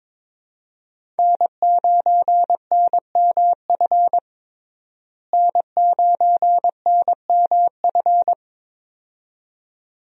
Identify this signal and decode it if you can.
Morse code, heard as a keyed tone, about 22 wpm, 700 Hz. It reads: N9NMF N9NMF
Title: USA Call Signs - 5 Characters - Repeated 2x - 22wpm